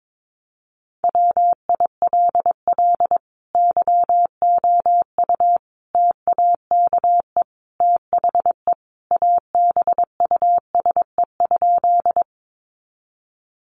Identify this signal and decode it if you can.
Morse code, heard as a keyed tone, about 22 wpm, 700 Hz. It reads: WILL YOU TAKE THE ABUSE?
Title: Sentences from Top 1300 Words 22wpm